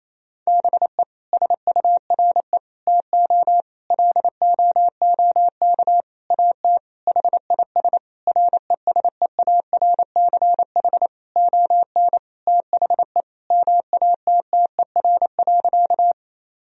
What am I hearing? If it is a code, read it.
Morse code, heard as a keyed tone, about 27 wpm, 700 Hz. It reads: BE SURE TO LOOK AT HIS RESEARCH ON THE MATTER.
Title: Sentences from Top 1000 Words - Review Part 4 - 28wpm